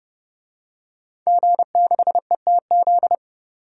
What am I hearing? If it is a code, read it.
Morse code, heard as a keyed tone, about 29 wpm, 700 Hz. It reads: G6ETZ
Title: DXCC-100 Call Signs - 5 Characters - 30wpm